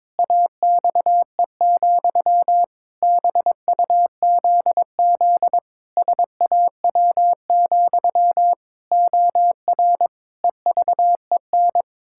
Morse code, heard as a keyed tone, about 22 wpm, 700 Hz. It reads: AXE, BUZZ SAW, OR EVEN
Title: Pangram Challenge v2 - 22wpm